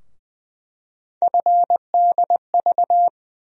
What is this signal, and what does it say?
Morse code, heard as a keyed tone, about 20 wpm, 700 Hz. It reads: FDV
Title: USA Call Sign Postfixes - 3 Letters 20wpm